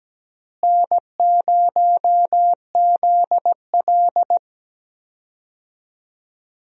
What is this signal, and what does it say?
Morse code, heard as a keyed tone, about 17 wpm, 700 Hz. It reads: N0ZL
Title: USA Call Signs - 4 Characters - Repeated 2x - 17wpm